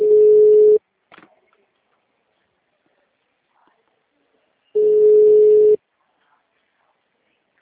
Ringback tone (Europe, 425 Hz).
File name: out- -2002- - - .48076